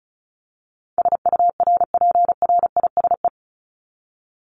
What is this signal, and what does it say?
Morse code, heard as a keyed tone, about 34 wpm, 700 Hz. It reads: SURPRISE
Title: Top 1751-2000 Words 35wpm